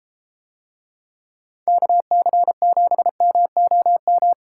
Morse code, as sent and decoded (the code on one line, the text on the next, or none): -.- -.-. --... -- --- --
KC7MOM